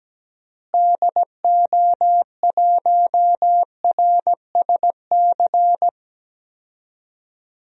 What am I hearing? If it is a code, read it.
Morse code, heard as a keyed tone, about 17 wpm, 700 Hz. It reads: DO1RSC
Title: DXCC-100 Call Signs - 6 Characters - Repeated 2x - 17wpm